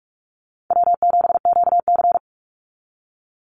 Morse code, heard as a keyed tone, about 44 wpm, 700 Hz. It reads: W7QC
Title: Easy Call Signs - 45wpm